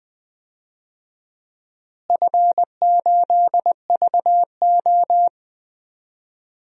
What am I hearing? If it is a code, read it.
Morse code, heard as a keyed tone, about 20 wpm, 700 Hz. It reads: F8VO
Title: DXCC-100 Call Signs - 4 Characters - Repeated 3x - 20wpm